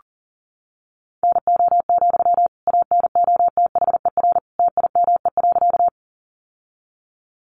Morse code, heard as a keyed tone, about 40 wpm, 700 Hz. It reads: NO, ANOTHER TIME.
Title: Sentences from Top 600 Words 40wpm